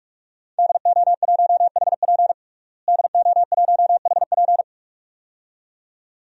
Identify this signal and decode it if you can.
Morse code, heard as a keyed tone, about 46 wpm, 700 Hz. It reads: DO1HP DO1HP
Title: DXCC-100 Call Signs - 5 Characters - Repeated 2x - 45wpm